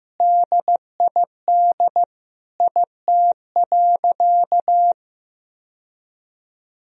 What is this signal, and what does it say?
Morse code, heard as a keyed tone, about 15 wpm, 700 Hz. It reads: DID IT.
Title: Sentences from Top 1000 Words - Review Part 4 - 15wpm